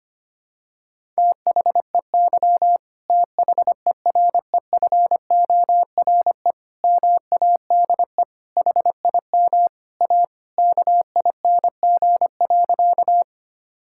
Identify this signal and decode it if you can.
Morse code, heard as a keyed tone, about 25 wpm, 700 Hz. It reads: THEY THEREFORE MADE HIM A KING.
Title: Sentences from Top 1000 Words - Review Part 5 - 25wpm